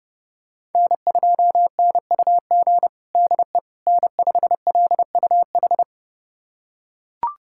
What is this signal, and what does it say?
Morse code, heard as a keyed tone, about 30 wpm, 700 Hz. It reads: N2NUG DE N5LUH
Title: USA Callsign DE Callsign - 5 Characters - 30wpm